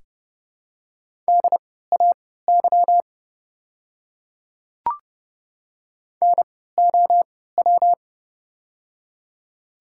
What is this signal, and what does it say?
Morse code, heard as a keyed tone, about 30 wpm, spaced as for 17 wpm, 700 Hz. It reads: DAY NOW
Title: 3x Character Spacing - Top 200 Words - 30wpm